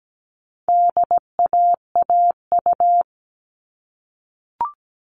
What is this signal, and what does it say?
Morse code, heard as a keyed tone, about 17 wpm, 700 Hz. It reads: DAAU
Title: All Cut Numbers - 4-Digit 17wpm